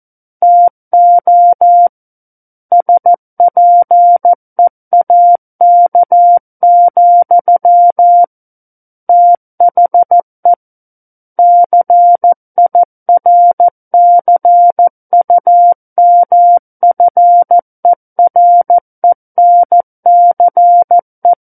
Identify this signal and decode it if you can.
Morse code, heard as a keyed tone, about 14 wpm, 700 Hz. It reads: TO SPEAK, THE CIRCUMFERENCE
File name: Book One / 05-Chapter 5 - 14 WPM